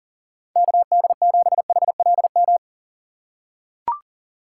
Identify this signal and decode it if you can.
Morse code, heard as a keyed tone, about 40 wpm, 700 Hz. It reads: KD7HLM
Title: USA Call Signs - 6 Characters - 4x Character Spacing - 40wpm